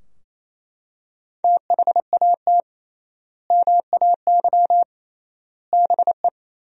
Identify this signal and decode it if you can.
Morse code, heard as a keyed tone, about 28 wpm, 700 Hz. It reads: THAT MAY BE
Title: 3x Word-Spacing - Sentences from Top 400 Words - 28wpm